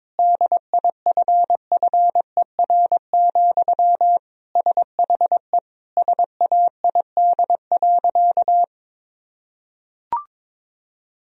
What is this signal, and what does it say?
Morse code, heard as a keyed tone, about 22 wpm, 700 Hz. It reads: DIFFER, SHE SAID.
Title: Sentences from Top 1000 Words - Review Part 1 - 22wpm